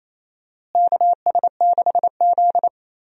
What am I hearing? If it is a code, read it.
Morse code, heard as a keyed tone, about 28 wpm, 700 Hz. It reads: KS6Z
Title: USA Call Signs - 4 Characters - Repeated 2x - 28wpm